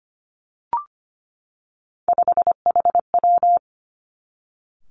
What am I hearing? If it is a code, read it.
Morse code, heard as a keyed tone, about 25 wpm, 700 Hz. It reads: 5HW
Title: Three Challenging Letter-Number Combinations 25wpm